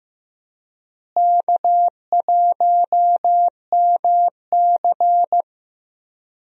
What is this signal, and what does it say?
Morse code, heard as a keyed tone, about 15 wpm, 700 Hz. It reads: K1MC
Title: USA Call Signs - 4 Characters - Repeated 3x - 15wpm